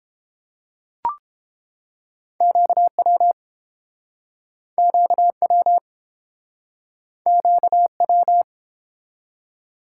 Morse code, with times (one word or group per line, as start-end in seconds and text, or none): 2.4-3.3 QW
4.8-5.8 QW
7.3-8.4 QW